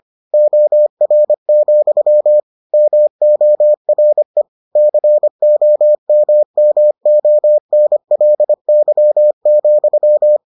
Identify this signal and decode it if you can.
Morse code, heard as a keyed tone, about 25 wpm, 600 Hz. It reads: OR, MORE COMMONLY,